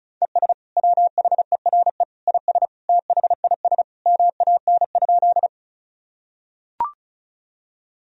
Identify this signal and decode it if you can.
Morse code, heard as a keyed tone, about 36 wpm, 690 Hz. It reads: ES WHERE IS THIS MAN?
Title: Sentences from Top 200 Words and Abbreviations 35wpm